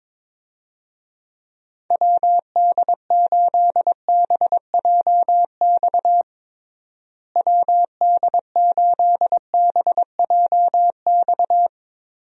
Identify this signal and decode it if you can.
Morse code, heard as a keyed tone, about 22 wpm, 700 Hz. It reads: WD8BJX WD8BJX
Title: USA Call Signs - 6 Characters - Repeated 3x - 22wpm